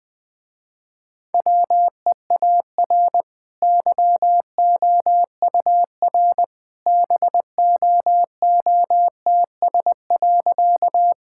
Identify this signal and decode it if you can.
Morse code, heard as a keyed tone, about 20 wpm, 700 Hz. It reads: WEAR YOUR BOOTS.
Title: Easy Sentences with Hints - 20wpm